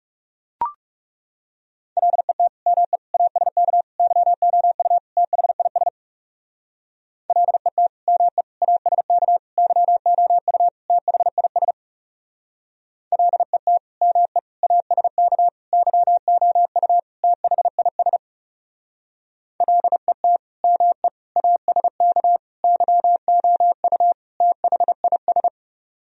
Morse code, heard as a keyed tone, about 40 wpm, 700 Hz. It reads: LET ME ASK YOU THIS LET ME ASK YOU THIS LET ME ASK YOU THIS LET ME ASK YOU THIS